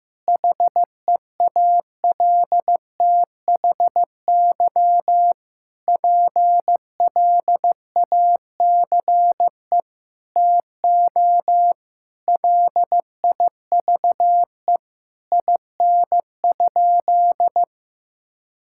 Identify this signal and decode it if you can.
Morse code, heard as a keyed tone, about 15 wpm, 700 Hz. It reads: HEALTHY PLACE TO LIVE IN?
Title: Sentences from Top 1000 Words - Review Part 5 - 15wpm